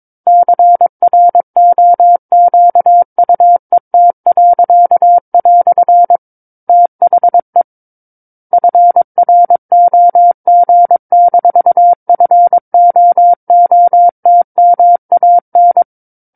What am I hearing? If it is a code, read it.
Morse code, heard as a keyed tone, about 22 wpm, 700 Hz. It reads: CROQUET." THE FROG-FOOTMAN
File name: 06-Chapter-06 22 WPM